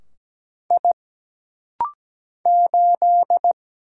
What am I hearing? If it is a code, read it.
Morse code, heard as a keyed tone, about 17 wpm, 700 Hz. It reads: I 8
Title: Single Letter-Number - Rapid-Fire 17wpm